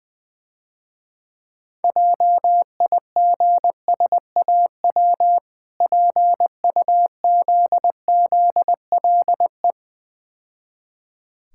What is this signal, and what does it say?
Morse code, heard as a keyed tone, about 20 wpm, 700 Hz. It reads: JIGSAW PUZZLE
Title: Difficult Words - Speed-Racing with Hints - 20wpm